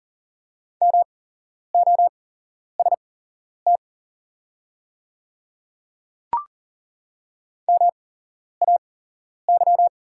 Morse code, heard as a keyed tone, about 40 wpm, spaced as for 11 wpm, 700 Hz. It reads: MOST MAY
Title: 8x Character Spacing - Top 100 Words - 40wpm